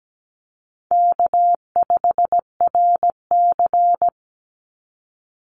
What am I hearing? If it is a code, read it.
Morse code, heard as a keyed tone, about 17 wpm, 700 Hz. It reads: K5RC
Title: Easy Call Signs Repeated 2x - 17wpm